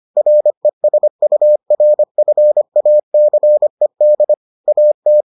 Morse code, heard as a keyed tone, about 25 wpm, 600 Hz. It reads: RESURFACED AT